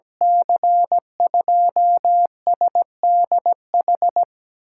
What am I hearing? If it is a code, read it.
Morse code, heard as a keyed tone, about 17 wpm, 700 Hz. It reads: C2SDH
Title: USA Call Signs - 6 Characters - 8x Character Spacing - 17wpm